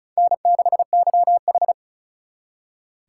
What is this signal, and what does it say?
Morse code, heard as a keyed tone, about 44 wpm, 700 Hz. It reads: N6YH
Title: Speed Racing - USA 1x2 Call Signs 35wpm